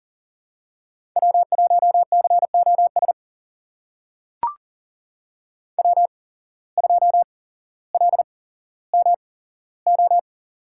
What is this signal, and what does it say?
Morse code, heard as a keyed tone, about 40 wpm, spaced as for 17 wpm, 700 Hz. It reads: W1COS W2LMO
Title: USA Call Signs - 5 Characters - 8x Character Spacing - 40wpm